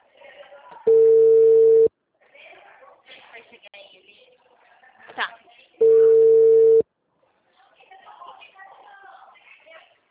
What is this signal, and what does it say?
Ringback tone (China, 450 Hz), with speech.